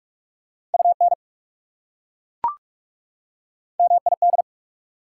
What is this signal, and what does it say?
Morse code, heard as a keyed tone, about 40 wpm, 690 Hz. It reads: UN MID